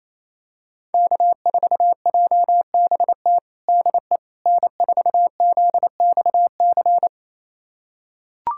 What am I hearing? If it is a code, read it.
Morse code, heard as a keyed tone, about 28 wpm, 700 Hz. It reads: K4JBT DE N4ZXC
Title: USA Callsign DE Callsign - 5 Characters - 28wpm